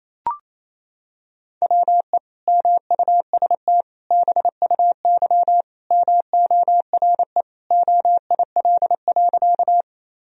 Morse code, heard as a keyed tone, about 28 wpm, 700 Hz. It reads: WE MUST BUY MORE OIL.
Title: Sentences from Top 1000 Words - Review Part 4 - 28wpm